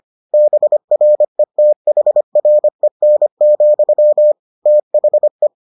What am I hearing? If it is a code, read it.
Morse code, heard as a keyed tone, about 25 wpm, 600 Hz. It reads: BRETHREN, THE